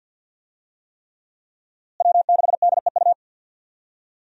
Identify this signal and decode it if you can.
Morse code, heard as a keyed tone, about 53 wpm, 700 Hz. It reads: W6DEU
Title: USA Call Signs - 5 Characters - Repeated 3x - 50wpm